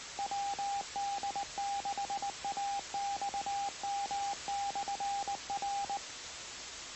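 Morse code, heard as a keyed tone, about 19 wpm, 800 Hz, with radio noise. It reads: WD6AXM/R
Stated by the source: Morse code heard on the 2 m band at 146.085 MHz.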